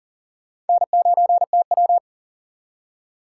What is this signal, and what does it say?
Morse code, heard as a keyed tone, about 52 wpm, 700 Hz. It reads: N9TW